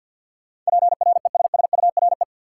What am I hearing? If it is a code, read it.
Morse code, heard as a keyed tone, about 48 wpm, 700 Hz. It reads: PRESSURE
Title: Sentences from Top 1000 Words - Review Part 6 - New Word Drill - 50wpm